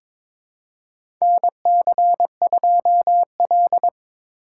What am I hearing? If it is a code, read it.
Morse code, heard as a keyed tone, about 22 wpm, 700 Hz. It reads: NC2L
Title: USA Call Signs - 4 Characters - 4x Character Spacing - 22wpm